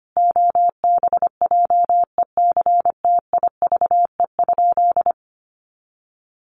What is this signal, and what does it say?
Morse code, heard as a keyed tone, about 25 wpm, 700 Hz. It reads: OBJECTIVE?
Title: Sentences from Top 1200 Words 25wpm